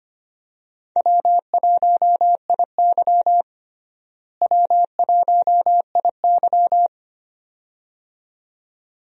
Morse code, heard as a keyed tone, about 25 wpm, 700 Hz. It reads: W1IY W1IY
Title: USA Call Signs - 4 Characters - Repeated 2x - 25wpm